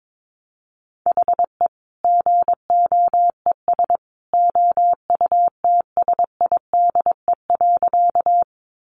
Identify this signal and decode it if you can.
Morse code, heard as a keyed tone, about 22 wpm, 700 Hz. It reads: HE GOES OUTSIDE.
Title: Additional Easy Phrases 22wpm